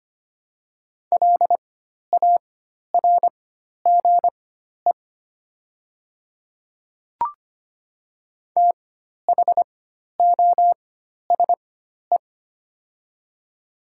Morse code, heard as a keyed tone, about 25 wpm, spaced as for 12 wpm, 700 Hz. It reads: LARGE THOSE
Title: 4x Character Spacing - Top 300 Words - 25wpm